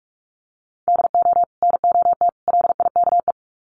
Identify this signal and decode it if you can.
Morse code, heard as a keyed tone, about 44 wpm, 700 Hz. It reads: DO NOT LIKE
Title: Sets of 3 Words 45wpm